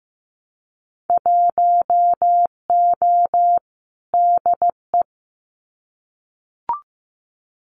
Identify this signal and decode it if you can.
Morse code, heard as a keyed tone, about 15 wpm, 700 Hz. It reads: 1O DE
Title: Winter Field Day Exchanges 15wpm